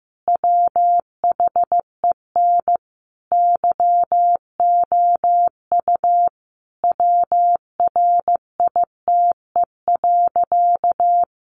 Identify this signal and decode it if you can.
Morse code, heard as a keyed tone, about 15 wpm, 700 Hz. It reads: WHEN YOU WRITE.
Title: Sentences from Top 800 Words 15wpm